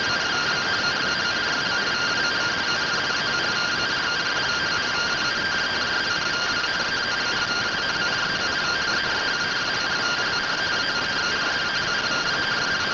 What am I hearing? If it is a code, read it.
Two-tone teleprinter signal, about 170 Hz shift.
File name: findings / CT AXIAL -111.gen-13 rtty rx